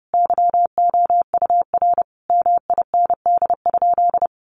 Morse code, heard as a keyed tone, about 30 wpm, 700 Hz. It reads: YOUR MIND?
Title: Sentences from Top 1400 Words 30wpm